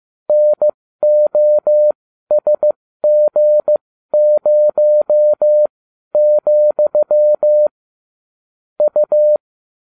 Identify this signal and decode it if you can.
Morse code, heard as a keyed tone, about 15 wpm, spaced as for 12 wpm, 600 Hz. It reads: NOSG0, U